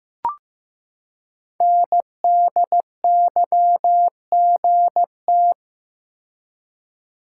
Morse code, heard as a keyed tone, about 15 wpm, 700 Hz. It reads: NDYGT
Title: Code Groups - 5 Letter:Numbers - 15wpm